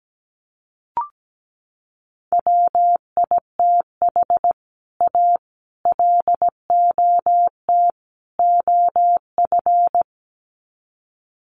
Morse code, heard as a keyed tone, about 17 wpm, 700 Hz. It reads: WITH A LOT OF
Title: Sets of 4 Words 17wpm